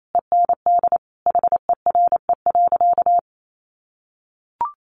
Morse code, heard as a keyed tone, about 28 wpm, 700 Hz. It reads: END HERE.
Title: Sentences from Top 1200 Words 28wpm